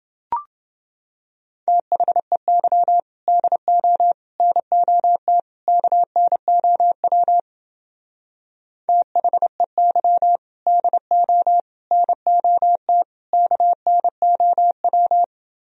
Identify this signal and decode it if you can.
Morse code, heard as a keyed tone, about 28 wpm, 700 Hz. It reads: THEY DO NOT KNOW THEY DO NOT KNOW